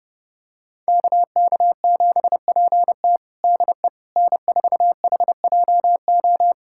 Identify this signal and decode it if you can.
Morse code, heard as a keyed tone, about 32 wpm, 700 Hz. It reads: KK7PT DE N4HJO
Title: USA Callsign DE Callsign - 5 Characters - 30wpm